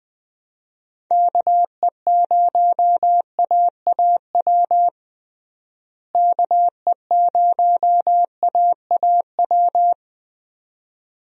Morse code, heard as a keyed tone, about 20 wpm, 700 Hz. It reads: KE0AAW KE0AAW
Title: USA Call Signs - 6 Characters - Repeated 2x - 20wpm